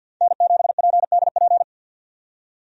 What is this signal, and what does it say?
Morse code, heard as a keyed tone, about 52 wpm, 700 Hz. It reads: N7PDP
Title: USA Call Signs - 5 Characters - 50wpm